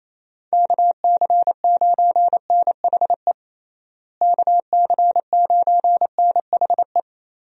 Morse code, heard as a keyed tone, about 28 wpm, 700 Hz. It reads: KC9NHE KC9NHE
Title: USA Call Signs - 6 Characters - Repeated 3x - 28wpm